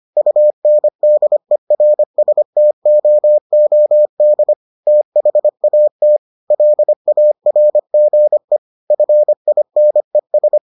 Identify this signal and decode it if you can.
Morse code, heard as a keyed tone, about 25 wpm, 600 Hz. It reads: UNDERSTOOD THAT LARGE FINES